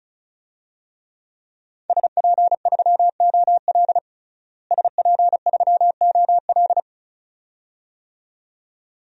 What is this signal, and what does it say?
Morse code, heard as a keyed tone, about 34 wpm, 700 Hz. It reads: SP3OL SP3OL
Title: DXCC-100 Call Signs - 5 Characters - Repeated 2x - 35wpm